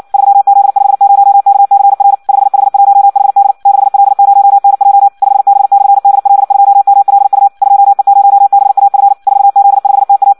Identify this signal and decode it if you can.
Morse code, heard as a keyed tone, about 56 wpm, 800 Hz. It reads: 3Z50KCR 5H1HD 6L0NJ 5B/DL2MLU 2E0BAF 4Z5TA
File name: Audio practica CW / qrq-tx-58 2